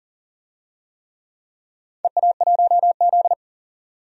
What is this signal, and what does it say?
Morse code, heard as a keyed tone, about 38 wpm, 700 Hz. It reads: EA1Z
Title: DXCC-100 Call Signs - 4 Characters - 40wpm